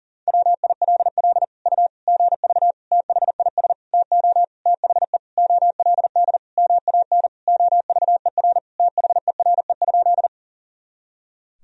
Morse code, heard as a keyed tone, about 40 wpm, 700 Hz. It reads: WILL U GV THIS TO THE OLD MAN OVER THERE?
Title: Sentences from Top 200 Words and Abbreviations 40wpm